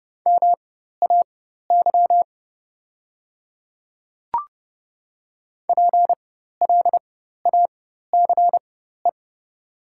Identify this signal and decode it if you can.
Morse code, heard as a keyed tone, about 30 wpm, spaced as for 14 wpm, 700 Hz. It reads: MAY PLACE
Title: 4x Character Spacing - Top 300 Words - 30wpm